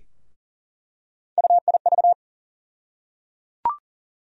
Morse code, heard as a keyed tone, about 37 wpm, 700 Hz. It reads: UIV
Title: USA Call Sign Postfixes - 3 Letters 40wpm